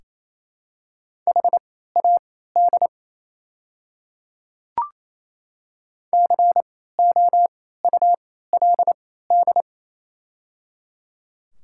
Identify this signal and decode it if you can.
Morse code, heard as a keyed tone, about 27 wpm, spaced as for 16 wpm, 700 Hz. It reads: HAD COULD
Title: 3x Character Spacing - Top 500 Words - 28wpm